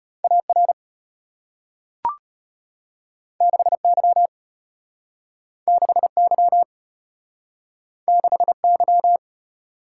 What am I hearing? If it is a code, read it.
Morse code, heard as a keyed tone, about 34 wpm, 700 Hz. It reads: AR 6Y 6Y 6Y